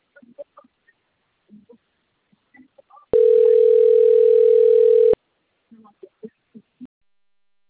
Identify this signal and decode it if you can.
Ringback tone (North America), 440 + 480 Hz.
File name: out- - - - - .76